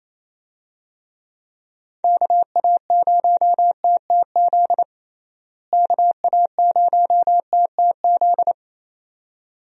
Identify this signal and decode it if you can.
Morse code, heard as a keyed tone, about 28 wpm, 700 Hz. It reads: KA0TTZ KA0TTZ
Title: USA Call Signs - 6 Characters - Repeated 2x - 28wpm